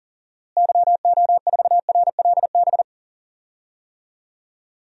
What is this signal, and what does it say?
Morse code, heard as a keyed tone, about 40 wpm, 700 Hz. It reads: YO4RLB
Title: DXCC-100 Call Signs - 6 Characters - Repeated 2x - 40wpm